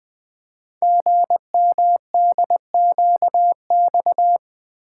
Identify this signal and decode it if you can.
Morse code, heard as a keyed tone, about 20 wpm, 700 Hz. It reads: GMDQX
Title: Code Groups - 5 Letters - 20wpm